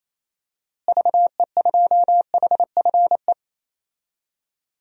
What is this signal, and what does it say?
Morse code, heard as a keyed tone, about 28 wpm, 700 Hz. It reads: VE2HFE
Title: DXCC-100 Call Signs - 6 Characters - Repeated 2x - 28wpm